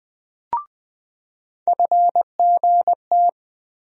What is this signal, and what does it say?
Morse code, heard as a keyed tone, about 20 wpm, 700 Hz. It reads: FGT
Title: USA Call Sign Postfixes - 3 Letters 20wpm